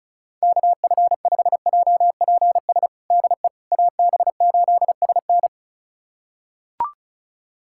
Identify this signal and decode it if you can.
Morse code, heard as a keyed tone, about 36 wpm, 700 Hz. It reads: KF5JPS DE AB8SN
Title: USA Callsign DE Callsign 35wpm